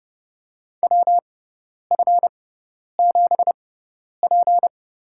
Morse code, heard as a keyed tone, about 30 wpm, spaced as for 10 wpm, 700 Hz. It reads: WF7P
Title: USA Call Signs - 6x Character Spacing - 30wpm